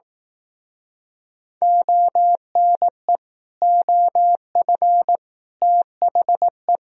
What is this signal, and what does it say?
Morse code, heard as a keyed tone, about 18 wpm, 700 Hz. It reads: ONE OF THE
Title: Speed-Racing - Sets of 4 Words 15wpm